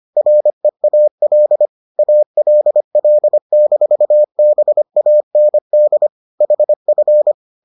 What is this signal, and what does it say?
Morse code, heard as a keyed tone, about 25 wpm, 600 Hz. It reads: REAL ALL-BAND HF